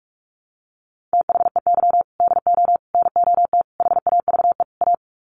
Morse code, heard as a keyed tone, about 44 wpm, 700 Hz. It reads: THEY DO NOT HAVE A
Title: Sets of 5 Words 45wpm